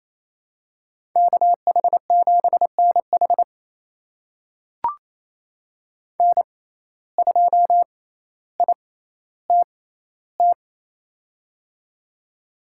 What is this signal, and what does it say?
Morse code, heard as a keyed tone, about 28 wpm, spaced as for 14 wpm, 700 Hz. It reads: KH7NH N2ITT
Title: USA Call Signs - 5 Characters - 6x Character Spacing - 28wpm